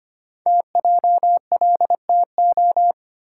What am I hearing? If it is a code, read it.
Morse code, heard as a keyed tone, about 25 wpm, 700 Hz. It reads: TJLTO
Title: Code Groups - 5 Letters - 25wpm